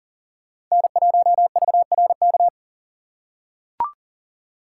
Morse code, heard as a keyed tone, about 40 wpm, 700 Hz. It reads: N1VRK
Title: USA Call Signs - 6x Character Spacing - 40wpm